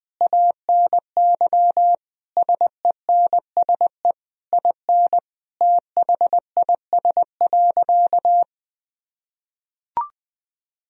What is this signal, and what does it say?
Morse code, heard as a keyed tone, about 20 wpm, 700 Hz. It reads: ANY SENSE IN THIS.
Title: Sentences from Top 1000 Words - Review Part 4 - 20wpm